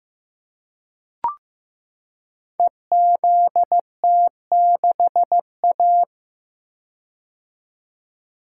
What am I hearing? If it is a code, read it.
Morse code, heard as a keyed tone, about 15 wpm, 700 Hz. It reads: EZT6A
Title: Code Groups - 5 Letter:Numbers - 15wpm